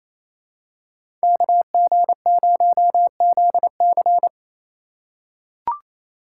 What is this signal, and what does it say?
Morse code, heard as a keyed tone, about 28 wpm, 700 Hz. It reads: KG0ZC
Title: USA Call Signs - 5 Characters - 2x Character Spacing - 28wpm